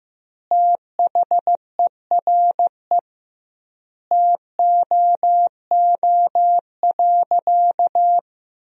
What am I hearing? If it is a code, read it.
Morse code, heard as a keyed tone, about 15 wpm, 700 Hz. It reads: THERE TOO.
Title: 2x Word-Spacing - Sentences from Top 200 Words - 15wpm